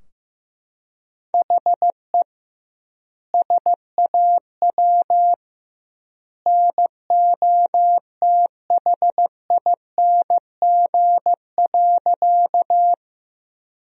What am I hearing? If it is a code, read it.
Morse code, heard as a keyed tone, about 15 wpm, 700 Hz. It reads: HE SAW NOTHING.
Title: 2x Word-Spacing - Sentences from Top 500 Words - 15wpm